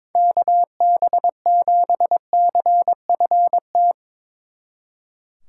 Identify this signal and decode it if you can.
Morse code, heard as a keyed tone, about 24 wpm, 700 Hz. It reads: KB7CFT
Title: Speed Racing - USA 2x3 Call Signs 22wpm